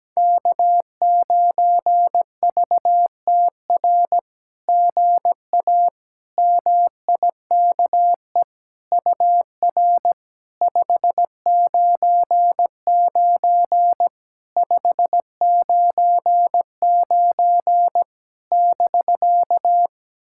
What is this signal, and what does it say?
Morse code, heard as a keyed tone, about 17 wpm, 700 Hz. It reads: K9VTR GA MIKE UR 599 599 <BK>
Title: SOTA QSO Practice 17wpm